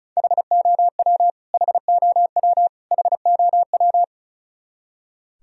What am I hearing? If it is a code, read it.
Morse code, heard as a keyed tone, about 34 wpm, 700 Hz. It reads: HOW HOW HOW
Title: Top 100 Words Repeated 35wpm